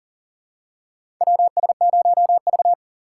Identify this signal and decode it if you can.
Morse code, heard as a keyed tone, about 40 wpm, 700 Hz. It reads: WS0V
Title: USA Call Signs - 3x Character Spacing - 40wpm